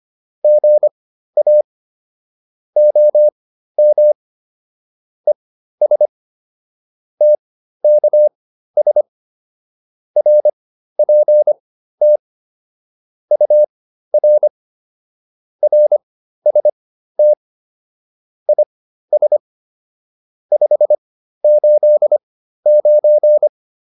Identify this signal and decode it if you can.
Morse code, heard as a keyed tone, about 25 wpm, spaced as for 13 wpm, 600 Hz. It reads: GA OM ES TKS RPT UR RST IS 589